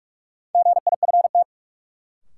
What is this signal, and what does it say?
Morse code, heard as a keyed tone, about 46 wpm, 700 Hz. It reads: GIFT
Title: Sentences from Top 900 Words - New Word Drill - 45wpm